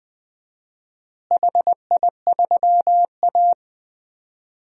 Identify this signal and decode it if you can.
Morse code, heard as a keyed tone, about 21 wpm, 700 Hz. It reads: HI3A
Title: DXCC-100 Call Signs - 4 Characters - Repeated 2x - 20wpm